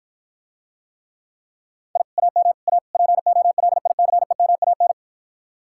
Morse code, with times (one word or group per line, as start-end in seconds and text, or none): 2.0-2.0 I
2.2-2.5 AM
2.7-2.8 A
2.9-4.9 POLICEMAN